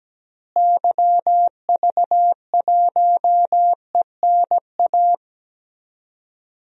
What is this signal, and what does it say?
Morse code, heard as a keyed tone, about 17 wpm, 700 Hz. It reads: YV1ENA
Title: DXCC-100 Call Signs - 6 Characters - Repeated 2x - 17wpm